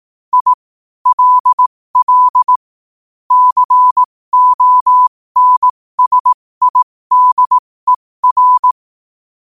Morse code, heard as a keyed tone, about 18 wpm, spaced as for 15 wpm, 1000 Hz. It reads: I LL CONSIDER
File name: qotdfiles / 15wpm